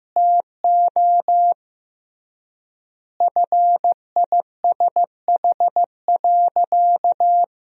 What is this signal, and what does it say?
Morse code, heard as a keyed tone, about 15 wpm, 700 Hz. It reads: TO FISH.
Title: 3x Word-Spacing - Sentences from Top 300 Words - 15wpm